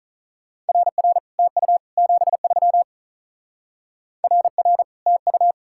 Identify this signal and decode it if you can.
Morse code, heard as a keyed tone, about 44 wpm, 700 Hz. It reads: RR TU 73 RR TU
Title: Speed-Racing Common QSO Phrases Encore - 35wpm